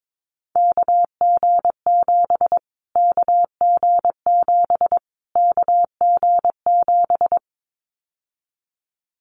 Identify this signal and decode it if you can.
Morse code, heard as a keyed tone, about 22 wpm, 700 Hz. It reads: KG7 KG7 KG7
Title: Callsign Prefixes for Top 25 DXCC - Repeated 3x - 22wpm